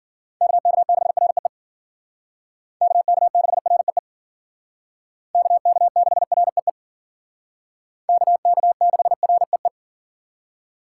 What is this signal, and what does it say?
Morse code, heard as a keyed tone, about 51 wpm, 700 Hz. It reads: KK6REE KK6REE KK6REE KK6REE